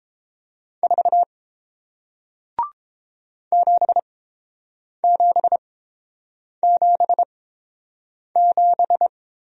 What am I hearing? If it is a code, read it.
Morse code, heard as a keyed tone, about 30 wpm, spaced as for 7 wpm, 700 Hz. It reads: 4 7777